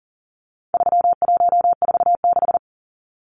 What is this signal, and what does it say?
Morse code, heard as a keyed tone, about 40 wpm, 700 Hz. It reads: 3146
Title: Numbers - 4-Digit 40wpm